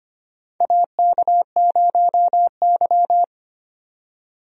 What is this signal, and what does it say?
Morse code, heard as a keyed tone, about 25 wpm, 700 Hz. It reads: AK0Y
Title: USA Call Signs - 4 Characters - 25wpm